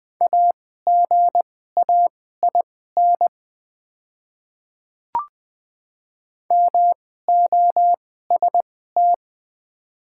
Morse code, heard as a keyed tone, about 20 wpm, spaced as for 15 wpm, 700 Hz. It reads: AGAIN MOST
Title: 2x Character Spacing - Top 200 Words - 20wpm